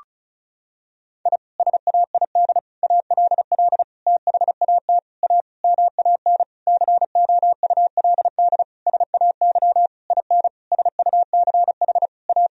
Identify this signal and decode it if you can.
Morse code, heard as a keyed tone, about 34 wpm, 700 Hz. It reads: I SAID ALL THAT A MAN COULD SAY IN SUCH A
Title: Sentences from Top 1000 Words - Review Part 2 - 35wpm